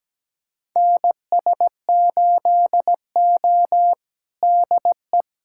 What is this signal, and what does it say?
Morse code, heard as a keyed tone, about 17 wpm, 700 Hz. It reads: NS8O DE
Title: USA Callsign DE Callsign - 4 Characters - 17wpm